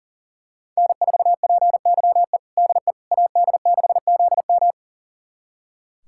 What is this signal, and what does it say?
Morse code, heard as a keyed tone, about 40 wpm, 700 Hz. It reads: N4PYE DE AD6ZM
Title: USA Callsign DE Callsign - 5 Characters - 40wpm